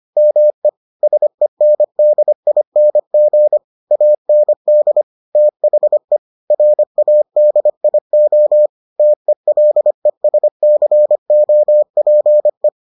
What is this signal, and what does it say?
Morse code, heard as a keyed tone, about 25 wpm, 600 Hz. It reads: ME SENDING AND THE RADIO TELESCOPE